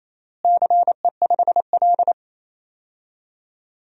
Morse code, heard as a keyed tone, about 28 wpm, 700 Hz. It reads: CE5L
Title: DXCC-100 Call Signs - 4 Characters - Repeated 3x - 28wpm